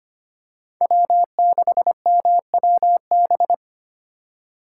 Morse code, heard as a keyed tone, about 25 wpm, 700 Hz. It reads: W6MWB
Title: USA Call Signs - 5 Characters - 2x Character Spacing - 25wpm